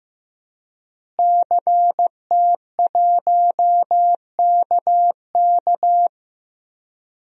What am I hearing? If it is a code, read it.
Morse code, heard as a keyed tone, about 15 wpm, 700 Hz. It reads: CT1KK
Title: DXCC-100 Call Signs - 5 Characters - Repeated 3x - 15wpm